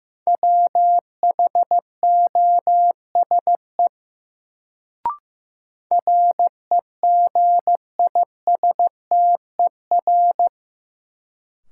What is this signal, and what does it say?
Morse code, heard as a keyed tone, about 15 wpm, 700 Hz. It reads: WHOSE REGISTER
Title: Sentences from Top 1200 Words - New Word Drill - 15wpm